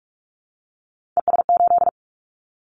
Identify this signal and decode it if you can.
Morse code, heard as a keyed tone, about 46 wpm, 700 Hz. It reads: ES8
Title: Callsign Prefixes for Top 50 DXCC - 45wpm